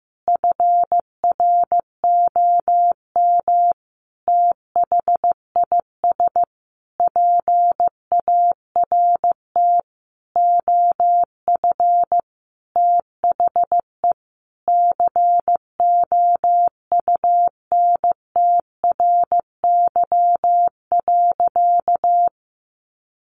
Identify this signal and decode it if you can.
Morse code, heard as a keyed tone, about 15 wpm, 700 Hz. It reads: FROM THIS PART OF THE COUNTRY.
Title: Sentences from Top 300 Words 15wpm